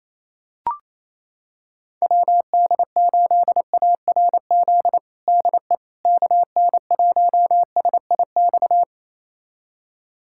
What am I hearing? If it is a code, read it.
Morse code, heard as a keyed tone, about 28 wpm, 700 Hz. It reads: WD8ARZ DE KN1SIX
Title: USA Callsign DE Callsign - 6 Characters - 28wpm